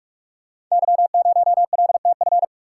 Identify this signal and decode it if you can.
Morse code, heard as a keyed tone, about 44 wpm, 700 Hz. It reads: Y0LTF
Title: Code Groups - 5 Letter:Numbers - 45wpm